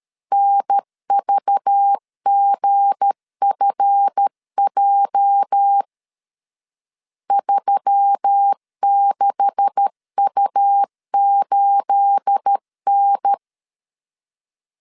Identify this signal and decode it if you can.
Morse code, heard as a keyed tone, about 13 wpm, 800 Hz. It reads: NVGFJ 36U8N
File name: LEZIONI / 9 286 12wpm 2305